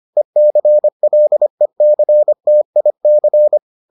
Morse code, heard as a keyed tone, about 25 wpm, 600 Hz. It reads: ECLECTIC